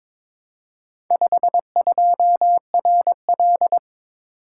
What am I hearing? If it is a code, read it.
Morse code, heard as a keyed tone, about 22 wpm, 700 Hz. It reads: 52RL
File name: practice / 22wpm / Four Letter-Number Combinations - 22wpm